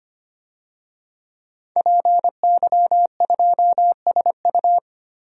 Morse code, heard as a keyed tone, about 25 wpm, 700 Hz. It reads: PY2SU
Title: DXCC-100 Call Signs - 5 Characters - Repeated 2x - 25wpm